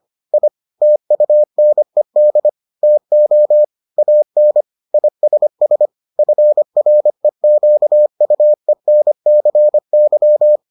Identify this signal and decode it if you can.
Morse code, heard as a keyed tone, about 25 wpm, 600 Hz. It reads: I TUNED TO AN ISS FREQUENCY